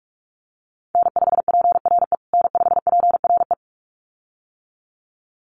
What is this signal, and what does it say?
Morse code, heard as a keyed tone, about 44 wpm, 700 Hz. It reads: N5PRE N5PRE
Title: USA Call Signs Repeated 2x 45wpm